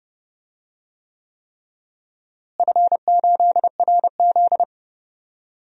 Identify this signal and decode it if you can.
Morse code, heard as a keyed tone, about 30 wpm, 700 Hz. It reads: F8RZ
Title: DXCC-100 Call Signs - 4 Characters - Repeated 3x - 30wpm